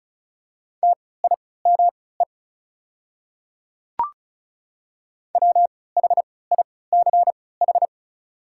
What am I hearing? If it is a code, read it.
Morse code, heard as a keyed tone, about 34 wpm, spaced as for 20 wpm, 690 Hz. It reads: TIME WHICH
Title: 3x Character Spacing - Top 200 Words - 35wpm